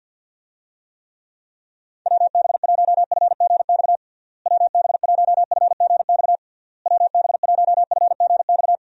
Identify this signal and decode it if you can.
Morse code, heard as a keyed tone, about 52 wpm, 700 Hz. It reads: WB1FGX WB1FGX WB1FGX
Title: USA Call Signs - 6 Characters - Repeated 3x - 50wpm